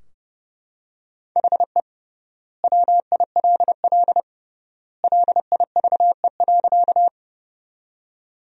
Morse code, heard as a keyed tone, about 32 wpm, 700 Hz. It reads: HE WILL LIVE.
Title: 3x Word-Spacing - Sentences from Top 200 Words - 30wpm